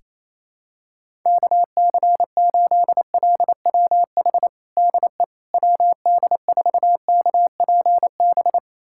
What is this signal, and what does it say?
Morse code, heard as a keyed tone, about 28 wpm, 700 Hz. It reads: KC8LWH DE WD4KPB
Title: USA Callsign DE Callsign 28wpm